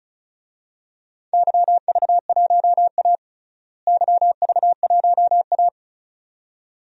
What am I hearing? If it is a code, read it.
Morse code, heard as a keyed tone, about 34 wpm, 700 Hz. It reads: YV1A YV1A
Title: DXCC-100 Call Signs - 4 Characters - Repeated 2x - 35wpm